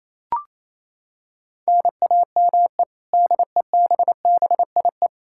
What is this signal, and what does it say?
Morse code, heard as a keyed tone, about 28 wpm, 700 Hz. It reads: NAME DEBBIE
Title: Practice QSO - Exchange 1 - Beginning - 28wpm